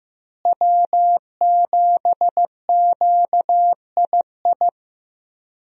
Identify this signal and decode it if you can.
Morse code, heard as a keyed tone, about 15 wpm, 700 Hz. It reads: W7QII
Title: USA Call Signs - 5 Characters - Repeated 2x - 15wpm